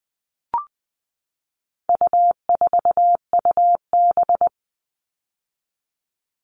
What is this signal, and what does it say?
Morse code, heard as a keyed tone, about 20 wpm, 700 Hz. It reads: U4UB
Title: All Cut Numbers - 4-Digit 20wpm